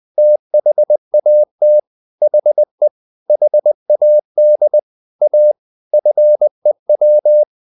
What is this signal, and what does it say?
Morse code, heard as a keyed tone, about 20 wpm, 600 Hz. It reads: THAT HE HAD A FEW